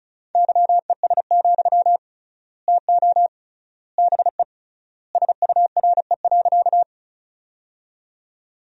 Morse code, heard as a keyed tone, about 36 wpm, 700 Hz. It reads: YES, TO BE SURE.
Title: 3x Word-Spacing - Sentences from Top 500 Words - 35wpm